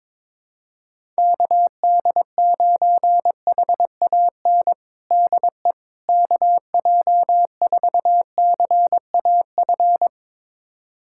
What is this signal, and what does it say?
Morse code, heard as a keyed tone, about 22 wpm, 700 Hz. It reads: KD9HAN DE KJ4CAF
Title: USA Callsign DE Callsign 22wpm